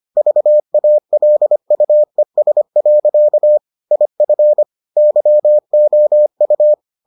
Morse code, heard as a keyed tone, about 25 wpm, 600 Hz. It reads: VALUES. IF YOU